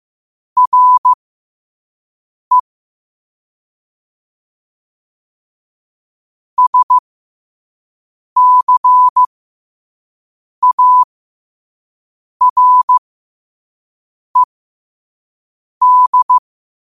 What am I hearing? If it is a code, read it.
Morse code, heard as a keyed tone, about 15 wpm, spaced as for 5 wpm, 1000 Hz. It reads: RE SCARED
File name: qotdfiles / 5wpm